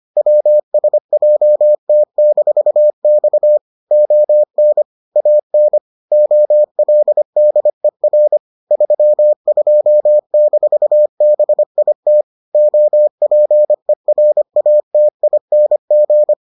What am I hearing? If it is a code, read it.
Morse code, heard as a keyed tone, about 25 wpm, 600 Hz. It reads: WSJT-X ON AN OLDER 32-BIT OPERATING